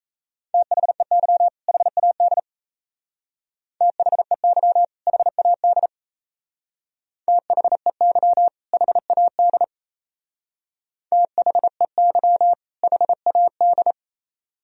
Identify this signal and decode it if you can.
Morse code, heard as a keyed tone, about 34 wpm, 700 Hz. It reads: THEY HAD THEY HAD THEY HAD THEY HAD
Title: Speed-Racing - Sets of 2 Words 28wpm